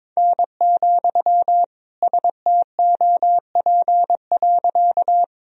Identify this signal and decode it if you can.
Morse code, heard as a keyed tone, about 21 wpm, 700 Hz. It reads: N, STOP.
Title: Sentences from Top 300 Words and Abbreviations 22wpm